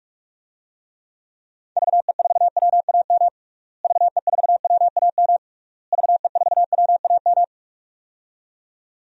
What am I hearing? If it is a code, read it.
Morse code, heard as a keyed tone, about 46 wpm, 700 Hz. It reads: VE4WAM VE4WAM VE4WAM
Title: DXCC-100 Call Signs - 6 Characters - Repeated 3x - 45wpm